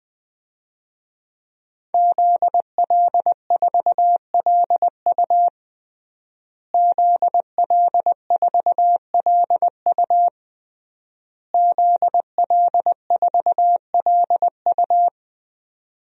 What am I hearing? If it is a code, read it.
Morse code, heard as a keyed tone, about 21 wpm, 700 Hz. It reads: ZL4LU ZL4LU ZL4LU
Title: DXCC-100 Call Signs - 5 Characters - Repeated 3x - 20wpm